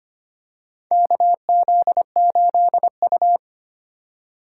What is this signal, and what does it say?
Morse code, heard as a keyed tone, about 25 wpm, 700 Hz. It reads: KZ8U
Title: USA Call Signs - 4 Characters - 3x Character Spacing - 25wpm